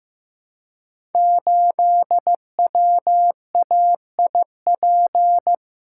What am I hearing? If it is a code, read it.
Morse code, heard as a keyed tone, about 15 wpm, 700 Hz. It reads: 8WAIP
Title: Code Groups - 5 Letter:Numbers - 15wpm